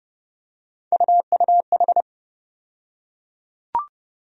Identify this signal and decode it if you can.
Morse code, heard as a keyed tone, about 30 wpm, 700 Hz. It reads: UUH
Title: USA Call Sign Postfixes - 3 Letters 30wpm